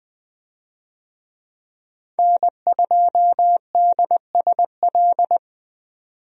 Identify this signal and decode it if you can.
Morse code, heard as a keyed tone, about 20 wpm, 700 Hz. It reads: N2DSL
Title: USA Call Signs - 5 Characters - 20wpm